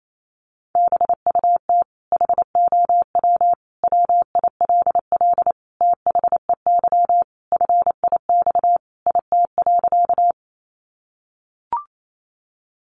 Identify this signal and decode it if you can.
Morse code, heard as a keyed tone, about 28 wpm, 700 Hz. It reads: BUT HOW WILL THEY FIX IT.
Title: Sentences from Top 1400 Words 28wpm